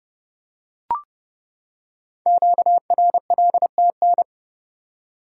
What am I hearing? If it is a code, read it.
Morse code, heard as a keyed tone, about 28 wpm, 700 Hz. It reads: QRLTN
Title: Code Groups - 5 Letters - 30wpm